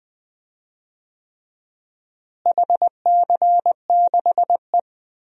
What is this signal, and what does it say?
Morse code, heard as a keyed tone, about 20 wpm, 700 Hz. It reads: HC6E
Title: DXCC-100 Call Signs - 4 Characters - Repeated 3x - 20wpm